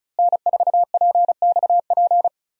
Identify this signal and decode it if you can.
Morse code, heard as a keyed tone, about 34 wpm, 700 Hz. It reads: N4PXP
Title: USA Call Signs - 5 Characters - Repeated 3x - 35wpm